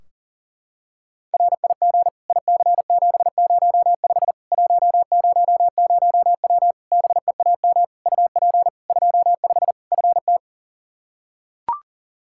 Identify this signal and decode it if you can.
Morse code, heard as a keyed tone, about 40 wpm, 700 Hz. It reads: RIG IC705 100W BEAM UP 25 FT
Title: K1USN SST QSO Practice 40wpm